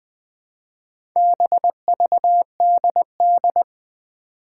Morse code, heard as a keyed tone, about 21 wpm, 700 Hz. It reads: BVDD
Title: Four Challenging Letter-Number Combinations 20wpm